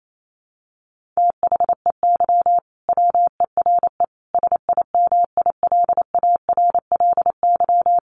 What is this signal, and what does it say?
Morse code, heard as a keyed tone, about 28 wpm, 700 Hz. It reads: THEY WERE SIMILARLY
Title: Sentences from Top 1400 Words 28wpm